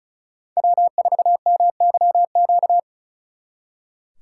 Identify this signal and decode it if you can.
Morse code, heard as a keyed tone, about 46 wpm, 700 Hz. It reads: W4MYQ